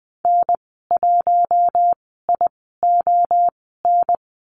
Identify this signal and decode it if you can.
Morse code, heard as a keyed tone, about 20 wpm, spaced as for 14 wpm, 700 Hz. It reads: N1ION
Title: 062 Lesson 11g - Letter h - Call Signs 20wpm